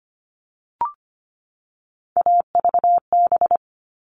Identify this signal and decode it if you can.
Morse code, heard as a keyed tone, about 25 wpm, 700 Hz. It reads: AVB
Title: All Cut Numbers - 3-Digit 25wpm